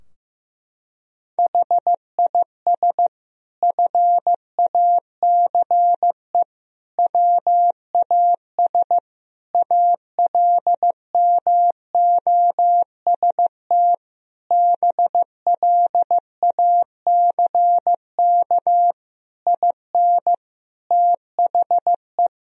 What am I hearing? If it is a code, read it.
Morse code, heard as a keyed tone, about 16 wpm, 700 Hz. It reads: HIS FACE WAS ALMOST BLACK IN THE
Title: Sentences from Top 1000 Words - Review Part 4 - 15wpm